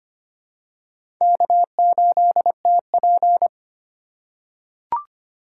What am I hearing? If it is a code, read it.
Morse code, heard as a keyed tone, about 25 wpm, 700 Hz. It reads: K8TP
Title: USA Call Signs - 4 Characters - 3x Character Spacing - 25wpm